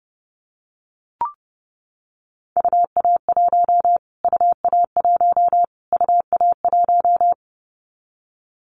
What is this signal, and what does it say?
Morse code, heard as a keyed tone, about 30 wpm, 700 Hz. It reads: UA1 UA1 UA1
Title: Callsign Prefixes for Top 25 DXCC - Repeated 3x - 30wpm